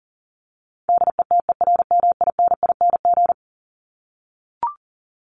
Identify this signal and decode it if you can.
Morse code, heard as a keyed tone, about 40 wpm, 700 Hz. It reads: DETERMINING
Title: ING Suffix - Top 100 Words with Base Words 40wpm